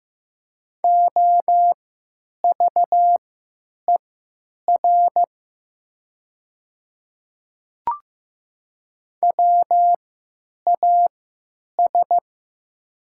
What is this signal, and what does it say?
Morse code, heard as a keyed tone, about 15 wpm, spaced as for 9 wpm, 690 Hz. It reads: OVER WAS
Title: 3x Character Spacing - Top 100 Words - 15wpm